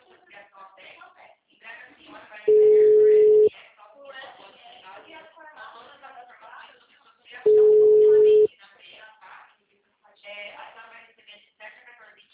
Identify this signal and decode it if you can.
Ringback tone (Europe, 425 Hz), with speech.